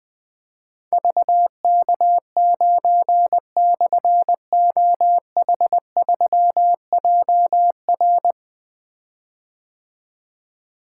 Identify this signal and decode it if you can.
Morse code, heard as a keyed tone, about 21 wpm, 700 Hz. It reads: VK9/OH3JR
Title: Difficult Call Signs 20wpm